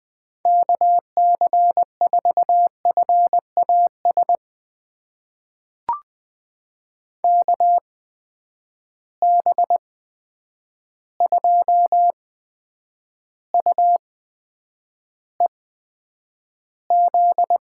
Morse code, heard as a keyed tone, about 20 wpm, spaced as for 9 wpm, 700 Hz. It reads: KC4FAS KB2UEZ
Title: USA Call Signs - 6 Characters - 8x Character Spacing - 20wpm